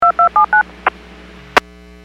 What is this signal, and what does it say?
DTMF, heard as telephone touch tones, tones of about 110 ms. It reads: 2279
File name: WPS / 02306 09-44 WPS 451.25 WPS